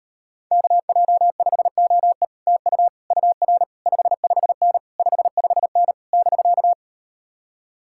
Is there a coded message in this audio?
Yes: Morse code from 0.5 to 6.7 s, heard as a keyed tone, about 38 wpm, 700 Hz. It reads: KJ5OE TU UR 55N 55N <BK>